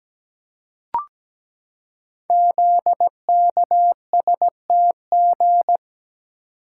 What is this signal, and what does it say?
Morse code, heard as a keyed tone, about 17 wpm, 700 Hz. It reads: ZKSTG
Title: Code Groups - 5 Letters - 17wpm